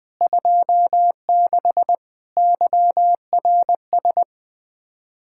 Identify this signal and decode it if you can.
Morse code, heard as a keyed tone, about 20 wpm, 700 Hz. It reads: 26 YRS
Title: Speed-Racing Common QSO Phrases - 17wpm